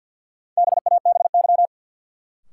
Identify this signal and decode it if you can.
Morse code, heard as a keyed tone, about 49 wpm, 700 Hz. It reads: BABY
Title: Sentences from Top 1300 Words - New Word Drill - 50wpm